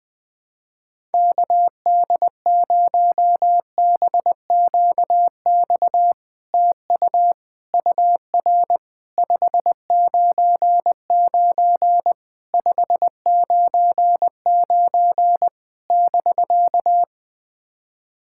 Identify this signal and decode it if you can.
Morse code, heard as a keyed tone, about 21 wpm, 700 Hz. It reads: KD0BQX TU UR 599 599 <BK>
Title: SOTA QSO Practice 20wpm